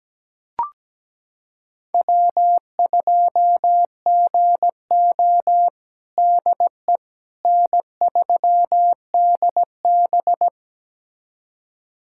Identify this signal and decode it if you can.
Morse code, heard as a keyed tone, about 17 wpm, 700 Hz. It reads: W2GO DE N3DB
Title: USA Callsign DE Callsign - 4 Characters - 17wpm